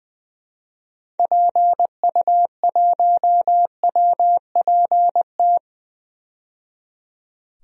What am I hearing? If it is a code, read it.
Morse code, heard as a keyed tone, about 20 wpm, 700 Hz. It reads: PU1WPT
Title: DXCC-100 Call Signs - 6 Characters - 20wpm